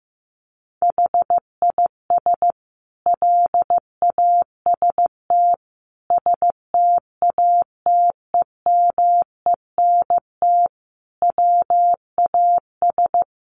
Morse code, heard as a keyed tone, about 15 wpm, 700 Hz. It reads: HIS LAST STATEMENT WAS
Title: Sentences from Top 1200 Words 15wpm